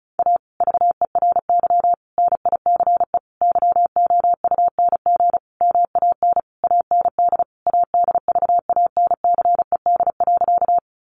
Morse code, heard as a keyed tone, about 34 wpm, 700 Hz. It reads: A VERY NICE YOUNG MAN AND ADVANCED.
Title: Sentences from Top 1200 Words 35wpm